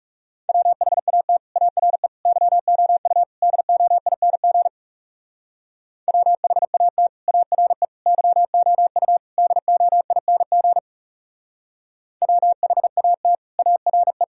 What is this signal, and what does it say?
Morse code, heard as a keyed tone, about 38 wpm, 700 Hz. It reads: WHAT ARE YOU DOING WHAT ARE YOU DOING WHAT ARE